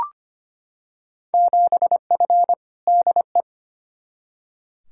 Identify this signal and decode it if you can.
Morse code, heard as a keyed tone, about 25 wpm, 700 Hz. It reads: 7F DE
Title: Field Day Exchanges Encore - 25wpm